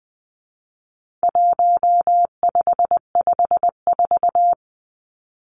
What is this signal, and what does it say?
Morse code, heard as a keyed tone, about 20 wpm, 700 Hz. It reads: 1554
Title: Numbers - 4-Digit 20wpm